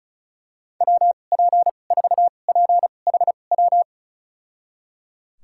Morse code, heard as a keyed tone, about 36 wpm, spaced as for 25 wpm, 700 Hz. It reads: WP4PHW
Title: USA Call Signs - 6 Characters - 2x Character Spacing - 35wpm